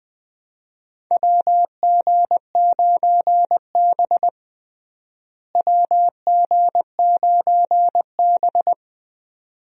Morse code, heard as a keyed tone, about 20 wpm, 700 Hz. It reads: WG9B WG9B
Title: USA Call Signs - 4 Characters - Repeated 2x - 20wpm